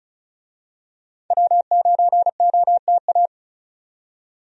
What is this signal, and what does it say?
Morse code, heard as a keyed tone, about 36 wpm, 700 Hz. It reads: W9OTA
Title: USA Call Signs - 5 Characters - 35wpm